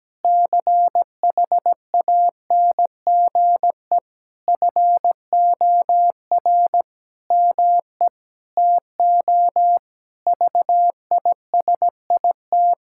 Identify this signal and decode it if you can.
Morse code, heard as a keyed tone, about 17 wpm, 700 Hz. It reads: CHANGE FOR ME TO VISIT
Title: Sentences from Top 1000 Words - Review Part 6 - 17wpm